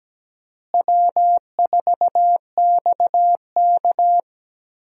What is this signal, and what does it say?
Morse code, heard as a keyed tone, about 17 wpm, 700 Hz. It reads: W4XK
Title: USA Call Signs - 4 Characters - Repeated 3x - 17wpm